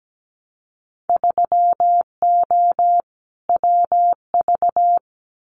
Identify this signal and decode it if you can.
Morse code, heard as a keyed tone, about 17 wpm, 700 Hz. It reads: 3O WV
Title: Winter Field Day Exchanges 17wpm